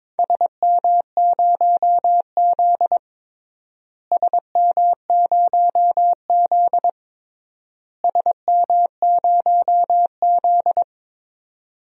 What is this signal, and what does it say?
Morse code, heard as a keyed tone, about 22 wpm, 700 Hz. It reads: SM0Z SM0Z SM0Z
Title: DXCC-100 Call Signs - 4 Characters - Repeated 3x - 22wpm